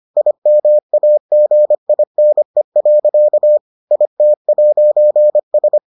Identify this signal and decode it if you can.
Morse code, heard as a keyed tone, about 25 wpm, 600 Hz. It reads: IMAGINE. IT'S